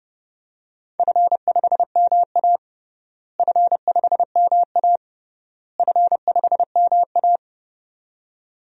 Morse code, heard as a keyed tone, about 30 wpm, 700 Hz. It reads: F5MA F5MA F5MA
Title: DXCC-100 Call Signs - 4 Characters - Repeated 3x - 30wpm